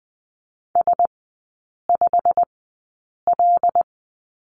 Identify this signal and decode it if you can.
Morse code, heard as a keyed tone, about 20 wpm, spaced as for 8 wpm, 690 Hz. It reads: S5L
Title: 087 Lesson 15e - Number 5 - Three Characters 20wpm